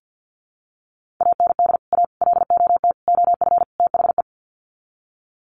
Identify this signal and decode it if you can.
Morse code, heard as a keyed tone, about 52 wpm, 700 Hz. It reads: AND A LOT OF THE
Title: Sets of 5 Words Encore 50wpm